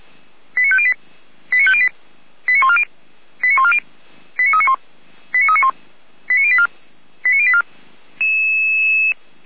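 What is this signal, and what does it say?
Five-tone selective call (ZVEI), heard as quick runs of five tones, tones of about 70 ms. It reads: 89508 89508 89140 89140 89391 89391 89084 89084
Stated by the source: Five-tone ZVEI selective call received by SDR.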